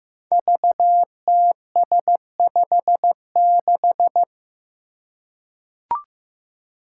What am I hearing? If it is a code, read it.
Morse code, heard as a keyed tone, about 15 wpm, 700 Hz. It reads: VTS56
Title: Five Challenging Letter-Number Combinations 15wpm